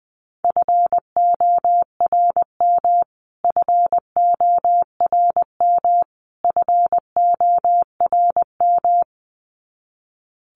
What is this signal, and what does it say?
Morse code, heard as a keyed tone, about 20 wpm, 700 Hz. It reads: FORM FORM FORM
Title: Top 500 Words Repeated 20wpm